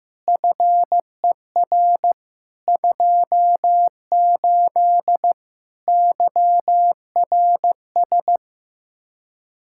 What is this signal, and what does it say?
Morse code, heard as a keyed tone, about 15 wpm, 700 Hz. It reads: FER 28 YRS
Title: Practice QSO - Exchange 3 - More Info - 15wpm